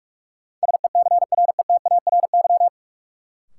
Morse code, heard as a keyed tone, about 41 wpm, 700 Hz. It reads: SECRETARY
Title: Sentences from Top 1300 Words - New Word Drill Alt - 45wpm